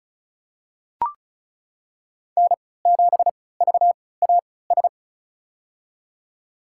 Morse code, heard as a keyed tone, about 34 wpm, spaced as for 20 wpm, 700 Hz. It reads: N7VAS
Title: USA Call Signs - 5 Characters - 3x Character Spacing - 35wpm